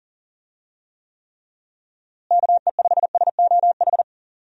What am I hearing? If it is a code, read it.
Morse code, heard as a keyed tone, about 40 wpm, 700 Hz. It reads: KE5SOH
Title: USA Call Signs - 6 Characters - Repeated 3x - 40wpm